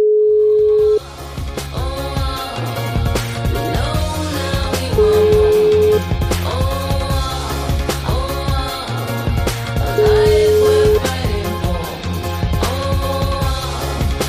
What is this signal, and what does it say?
Ringback tone (Europe, 425 Hz), with noise.